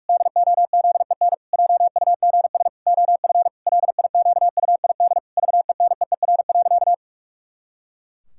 Morse code, heard as a keyed tone, about 41 wpm, 700 Hz. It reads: DOZEN JUGS OF LIQUID VENEER.
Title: Pangram Challenge v2 - 45wpm